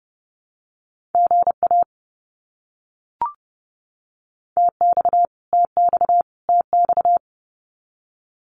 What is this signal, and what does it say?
Morse code, heard as a keyed tone, about 30 wpm, 700 Hz. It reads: GA TX TX TX
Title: Abbreviations Repeated 3x - 30wpm